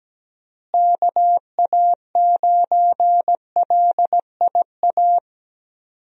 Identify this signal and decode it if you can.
Morse code, heard as a keyed tone, about 17 wpm, 700 Hz. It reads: KA9LIA
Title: USA Call Signs - 6 Characters - Repeated 2x - 17wpm